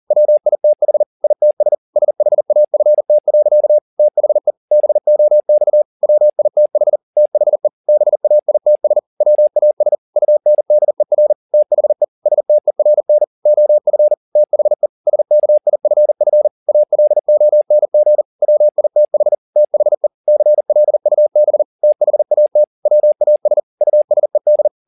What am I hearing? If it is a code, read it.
Morse code, heard as a keyed tone, about 40 wpm, 600 Hz. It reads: WITH ITS SHAFT. THE BOX WITH THE BAITS WAS UNDER THE STERN OF THE SKIFF ALONG WITH THE CLUB THAT WAS USED